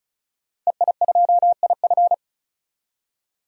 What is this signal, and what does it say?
Morse code, heard as a keyed tone, about 34 wpm, 700 Hz. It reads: EI2IF
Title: DXCC-100 Call Signs - 5 Characters - 35wpm